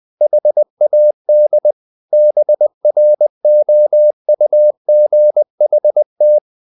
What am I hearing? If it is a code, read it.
Morse code, heard as a keyed tone, about 20 wpm, 600 Hz. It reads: HAD BROUGHT
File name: the old man and the sea / the old man and the sea part 1 20wpm